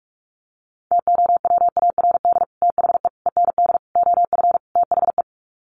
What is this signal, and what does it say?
Morse code, heard as a keyed tone, about 44 wpm, 700 Hz. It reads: TOWARD THE END OF THE
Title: Sets of 5 Words 45wpm